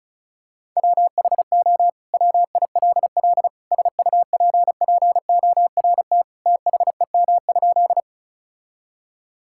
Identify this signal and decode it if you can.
Morse code, heard as a keyed tone, about 36 wpm, 700 Hz. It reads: WHO WILL SUPPORT THEM?
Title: Sentences from Top 1000 Words - Review Part 5 - 35wpm